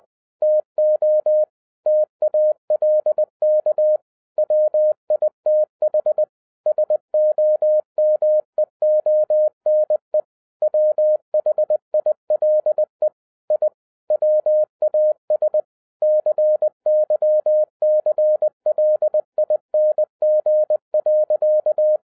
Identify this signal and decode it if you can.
Morse code, heard as a keyed tone, about 20 wpm, 610 Hz. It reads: TO TALK WITH SOMEONE WHILE I WAS CYCLING.
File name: 2012-01-mp3 / n05-CAME BACK TO HAM RADIO AFTER 30 YEARS OF INTERVAL-20wpm